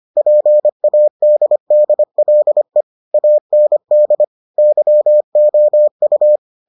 Morse code, heard as a keyed tone, about 25 wpm, 600 Hz. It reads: PADDLE AND YOU